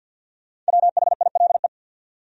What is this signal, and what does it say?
Morse code, heard as a keyed tone, about 66 wpm, spaced as for 58 wpm, 700 Hz. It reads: WHILE
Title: Speed-Racing Top 300 Words 50wpm